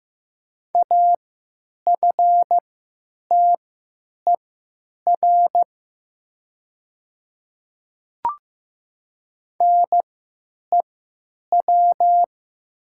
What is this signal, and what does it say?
Morse code, heard as a keyed tone, about 15 wpm, spaced as for 9 wpm, 700 Hz. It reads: AFTER NEW
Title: 3x Character Spacing - Top 200 Words - 15wpm